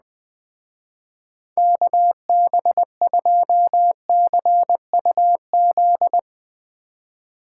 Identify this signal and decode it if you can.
Morse code, heard as a keyed tone, about 20 wpm, 700 Hz. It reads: KB2CUZ
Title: Speed Racing - USA 2x3 Call Signs 17wpm